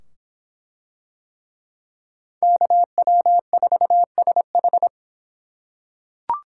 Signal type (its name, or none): Morse code